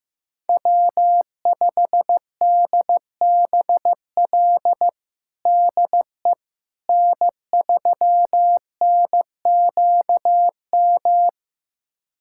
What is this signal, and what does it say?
Morse code, heard as a keyed tone, about 15 wpm, 700 Hz. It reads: W5DBL DE N3NQM
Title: USA Callsign DE Callsign - 5 Characters - 15wpm